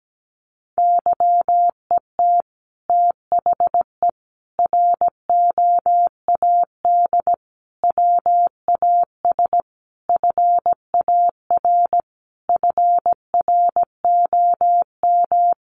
Morse code, heard as a keyed tone, about 17 wpm, 700 Hz. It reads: YET THE ROAD WAS FAR FROM
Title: Sentences from Top 1400 Words 17wpm